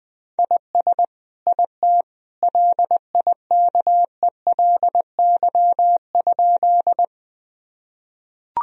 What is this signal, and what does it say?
Morse code, heard as a keyed tone, about 20 wpm, 700 Hz. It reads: IS IT LIKELY?
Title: Sentences from Top 1000 Words - Review Part 4 - 20wpm